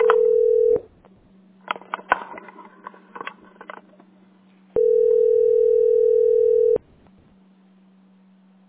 Ringback tone (North America), 440 + 480 Hz.
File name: 17fc2dde-e349-411a-8a15